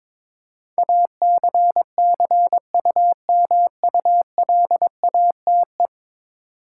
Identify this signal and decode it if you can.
Morse code, heard as a keyed tone, about 23 wpm, 700 Hz. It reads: ACCUMULATE
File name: practice / 22wpm / 10 Letter Words - First 3 Letters Given - 22wpm